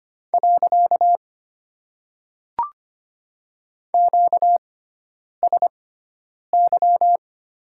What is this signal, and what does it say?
Morse code, heard as a keyed tone, about 25 wpm, 700 Hz. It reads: . Q S Y
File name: Top 90 QSO Elements - 6x Character Spacing - 25wpm